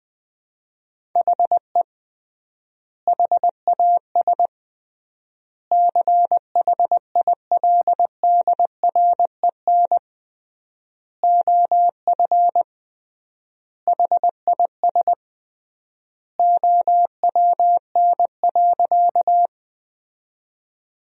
Morse code, heard as a keyed tone, about 20 wpm, 700 Hz. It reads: HE HAS CHILDREN OF HIS OWN.
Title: 3x Word-Spacing - Sentences from Top 300 Words - 20wpm